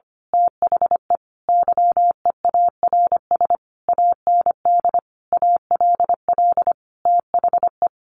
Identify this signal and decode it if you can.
Morse code, heard as a keyed tone, about 25 wpm, 700 Hz. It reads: THE YEARS AND ALL THE
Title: A Sound Of Thunder 25wpm